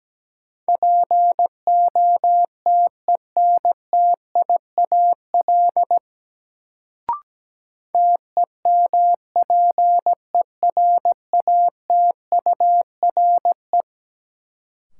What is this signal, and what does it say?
Morse code, heard as a keyed tone, about 17 wpm, 700 Hz. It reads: POTENTIAL TEMPERATURE
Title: Sentences from Top 1000 Words - Review Part 4 - New Word Drill - 17wpm